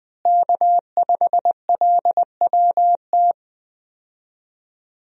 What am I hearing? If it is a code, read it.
Morse code, heard as a keyed tone, about 20 wpm, 700 Hz. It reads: K5LWT
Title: USA Call Signs - 5 Characters - Repeated 3x - 20wpm